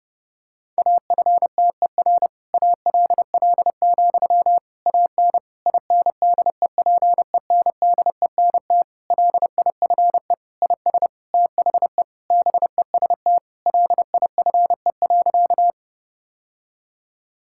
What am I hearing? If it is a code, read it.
Morse code, heard as a keyed tone, about 32 wpm, 700 Hz. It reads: AFTER ALL, AN INDEPENDENT LIFE IS THE BEST LIFE.
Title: Sentences from Top 1000 Words - Review Part 6 - 30wpm